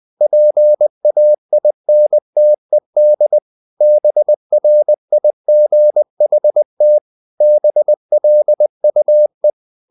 Morse code, heard as a keyed tone, about 20 wpm, 600 Hz. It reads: PAINTED BRIGHT BLUE
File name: the old man and the sea part 2 20wpm